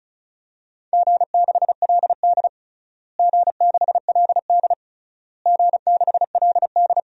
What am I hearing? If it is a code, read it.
Morse code, heard as a keyed tone, about 36 wpm, 700 Hz. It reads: G6LD G6LD G6LD
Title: DXCC-100 Call Signs - 4 Characters - Repeated 3x - 35wpm